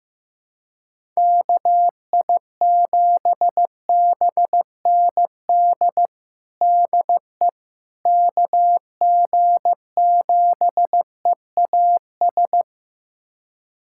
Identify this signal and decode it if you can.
Morse code, heard as a keyed tone, about 15 wpm, 700 Hz. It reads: KI7BND DE KG7EAS
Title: USA Callsign DE Callsign - 6 Characters - 15wpm